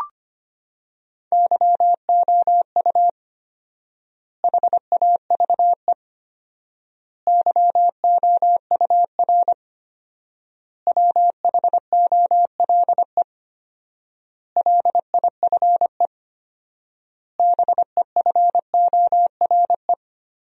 Morse code, heard as a keyed tone, about 25 wpm, 700 Hz. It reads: YOU HAVE YOUR WHOLE LIFE BEFORE
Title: 4x Word-Spacing - Sentences from Top 400 Words - 25wpm